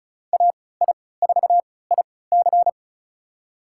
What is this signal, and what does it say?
Morse code, heard as a keyed tone, about 34 wpm, spaced as for 20 wpm, 700 Hz. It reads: AI4IC
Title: USA Call Signs - 5 Characters - 3x Character Spacing - 35wpm